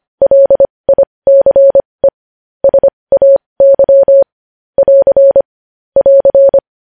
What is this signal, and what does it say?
Morse code, heard as a keyed tone, about 25 wpm, spaced as for 20 wpm, 550 Hz. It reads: LICE SAY <AR> <AR>